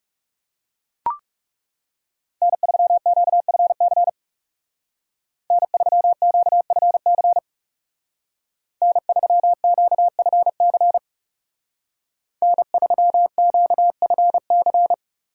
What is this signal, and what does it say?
Morse code, heard as a keyed tone, about 39 wpm, 700 Hz. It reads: N3QFC N3QFC N3QFC N3QFC